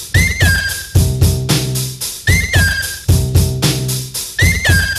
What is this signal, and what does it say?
Emergency Alert System (SAME) tones (header bursts), heard as harsh buzzing bursts. It is heard from 0.4 to 4.8 s.